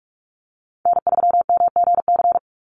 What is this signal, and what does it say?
Morse code, heard as a keyed tone, about 44 wpm, 700 Hz. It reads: N3MGC
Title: USA Call Signs 45wpm